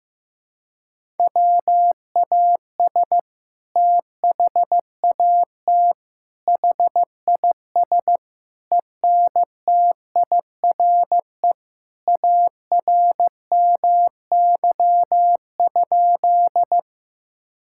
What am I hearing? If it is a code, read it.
Morse code, heard as a keyed tone, about 15 wpm, 700 Hz. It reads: WAS THAT HIS ENTIRE ARMY?
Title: Sentences from Top 1000 Words - Review Part 5 - 15wpm